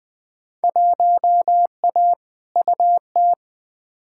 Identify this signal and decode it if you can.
Morse code, heard as a keyed tone, about 20 wpm, 700 Hz. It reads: 1A UT
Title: Field Day Exchanges - 20wpm